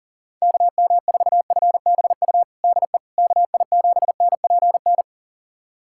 Morse code, heard as a keyed tone, about 40 wpm, 700 Hz. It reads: KM4FBU DE KI7NPN
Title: USA Callsign DE Callsign 40wpm